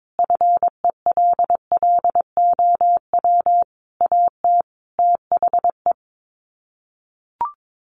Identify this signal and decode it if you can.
Morse code, heard as a keyed tone, about 22 wpm, 700 Hz. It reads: FELLOW AT THE
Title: Sets of 5 Words Encore 22wpm